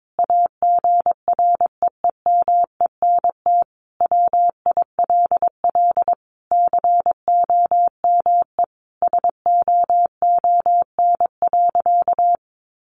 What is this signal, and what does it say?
Morse code, heard as a keyed tone, about 22 wpm, 700 Hz. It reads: AGREEMENT WILL COME SOON.
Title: Sentences from Top 1500 Words 22wpm